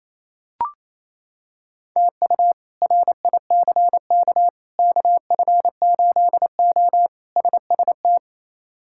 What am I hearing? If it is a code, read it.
Morse code, heard as a keyed tone, about 28 wpm, 700 Hz. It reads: TU RICK KF8O SST
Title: K1USN SST QSO Practice - 28wpm